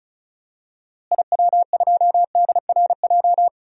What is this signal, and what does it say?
Morse code, heard as a keyed tone, about 34 wpm, 700 Hz. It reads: IW2DRJ
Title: DXCC-100 Call Signs - 6 Characters - Repeated 3x - 35wpm